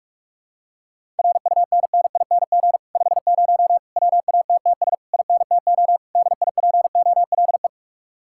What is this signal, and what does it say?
Morse code, heard as a keyed tone, about 49 wpm, 700 Hz. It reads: RUNNING 50 WATTS INTO DIPOLE